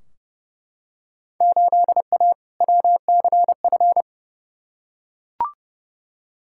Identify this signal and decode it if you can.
Morse code, heard as a keyed tone, about 30 wpm, 700 Hz. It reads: 8A WCF
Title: Field Day Exchanges Encore - 30wpm